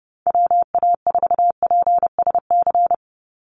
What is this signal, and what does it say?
Morse code, heard as a keyed tone, about 30 wpm, 700 Hz. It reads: WA4PSC
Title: Call Signs Repeated 2x - 30wpm